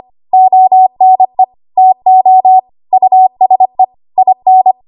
Morse code, heard as a keyed tone, about 25 wpm, 750 Hz. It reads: ONE TO USE IN
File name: Archive / 25 WPM / 25WPM